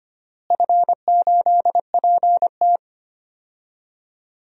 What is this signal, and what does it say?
Morse code, heard as a keyed tone, about 25 wpm, 700 Hz. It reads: F8PT
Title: DXCC-100 Call Signs - 4 Characters - Repeated 3x - 25wpm